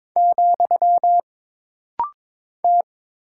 Morse code, heard as a keyed tone, about 22 wpm, 700 Hz. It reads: , T
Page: This rapid-fire timing is geared to push your ICR (Instant Character Recognition) skills.